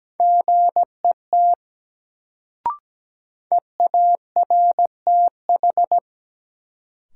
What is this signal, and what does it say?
Morse code, heard as a keyed tone, about 17 wpm, 700 Hz. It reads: GET EARTH
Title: Sentences from Top 200 Words - New Word Drill - 17wpm